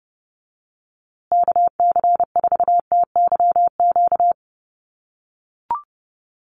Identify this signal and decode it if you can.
Morse code, heard as a keyed tone, about 30 wpm, 700 Hz. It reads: KC4TYQ
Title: USA Call Signs Repeated 3x 30wpm